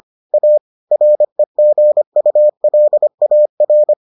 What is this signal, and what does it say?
Morse code, heard as a keyed tone, about 25 wpm, 600 Hz. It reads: A REGULAR